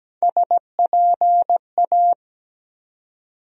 Morse code, heard as a keyed tone, about 17 wpm, 700 Hz. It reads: SPA
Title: USA Call Sign Postfixes - 3 Letters 17wpm